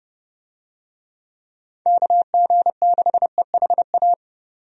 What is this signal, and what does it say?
Morse code, heard as a keyed tone, about 30 wpm, 700 Hz. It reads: KG6EHA
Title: USA Call Signs - 6 Characters - 30wpm